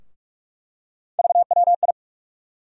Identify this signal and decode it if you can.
Morse code, heard as a keyed tone, about 46 wpm, 700 Hz. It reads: VWI
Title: USA Call Sign Postfixes - 3 Letters 45wpm